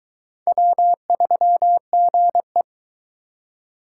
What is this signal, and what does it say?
Morse code, heard as a keyed tone, about 24 wpm, 700 Hz. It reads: W3GE